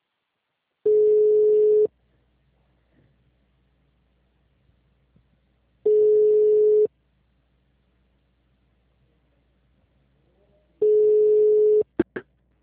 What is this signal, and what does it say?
Ringback tone (Europe, 425 Hz).
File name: out- -2002- - - .332